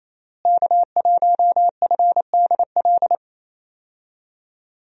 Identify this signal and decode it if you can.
Morse code, heard as a keyed tone, about 28 wpm, 700 Hz. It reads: K1FDL
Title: USA Call Signs - 5 Characters - Repeated 2x - 28wpm